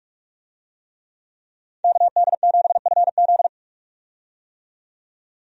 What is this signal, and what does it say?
Morse code, heard as a keyed tone, about 47 wpm, 700 Hz. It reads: KD7FZ
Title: USA Call Signs - 5 Characters - 45wpm